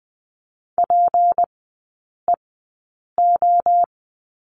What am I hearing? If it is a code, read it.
Morse code, heard as a keyed tone, about 20 wpm, spaced as for 8 wpm, 690 Hz. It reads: PEO
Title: 212 Lesson 34e - Number 8 - Three Characters 20wpm